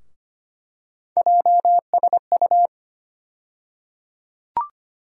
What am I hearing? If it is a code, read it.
Morse code, heard as a keyed tone, about 25 wpm, 700 Hz. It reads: JSU
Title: USA Call Sign Postfixes - 3 Letters 25wpm